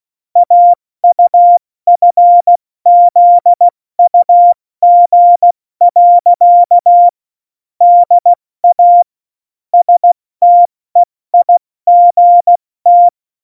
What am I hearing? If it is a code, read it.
Morse code, heard as a keyed tone, about 16 wpm, 700 Hz. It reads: AUFZUG. DA STEIGT
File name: Morsekurs / Witze / Witz 07. November 2017, 70 F80